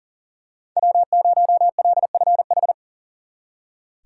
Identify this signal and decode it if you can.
Morse code, heard as a keyed tone, about 38 wpm, 700 Hz. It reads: W0LFH
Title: Speed Racing - USA Call Signs 40wpm